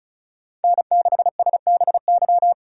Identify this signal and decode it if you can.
Morse code, heard as a keyed tone, about 36 wpm, 700 Hz. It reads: N6SBY
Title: USA Call Signs - 5 Characters - 2x Character Spacing - 35wpm